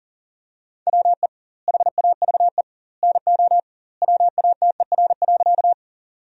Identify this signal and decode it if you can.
Morse code, heard as a keyed tone, about 40 wpm, 700 Hz. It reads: WE HAVE NO WATER.
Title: 2x Word-Spacing - Sentences from Top 100 Words - 40wpm